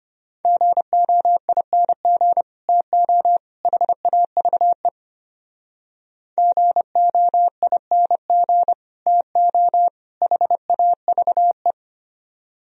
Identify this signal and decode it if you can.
Morse code, heard as a keyed tone, about 33 wpm, 700 Hz. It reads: GOING TO HAVE GOING TO HAVE